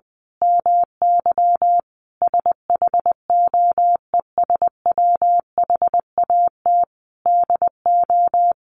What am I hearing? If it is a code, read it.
Morse code, heard as a keyed tone, about 19 wpm, 700 Hz. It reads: MY SHOES WHAT DO
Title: A Sound Of Thunder 20wpm